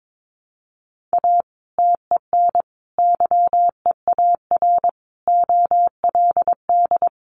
Morse code, heard as a keyed tone, about 22 wpm, 700 Hz. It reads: A TEN YEAR OLD
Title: Sentences from Top 1300 Words 22wpm